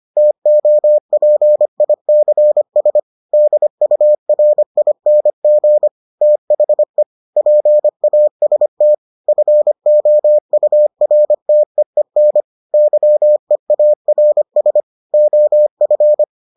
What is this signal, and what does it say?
Morse code, heard as a keyed tone, about 25 wpm, 600 Hz. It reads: TOPICS DURING THE PAST FOURTEEN YEARS OF